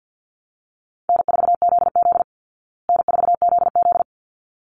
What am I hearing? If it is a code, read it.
Morse code, heard as a keyed tone, about 50 wpm, 700 Hz. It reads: N4ZZ N4ZZ
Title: Call Signs Repeated 3x - 50wpm